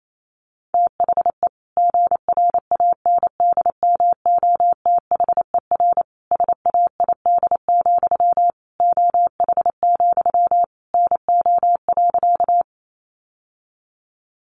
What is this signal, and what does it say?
Morse code, heard as a keyed tone, about 28 wpm, 700 Hz. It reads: THE GRANDMOTHER SAID, OH, NO.
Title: Sentences from Top 1200 Words 28wpm